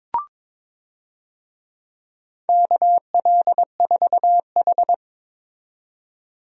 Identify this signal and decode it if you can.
Morse code, heard as a keyed tone, about 22 wpm, 700 Hz. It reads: KL4H
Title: USA Call Signs - 4 Characters - 22wpm